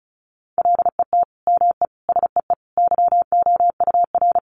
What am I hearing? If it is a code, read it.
Morse code, heard as a keyed tone, about 34 wpm, 700 Hz. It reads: LET ME SEE YOUR
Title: Sentences from Top 500 Words 35wpm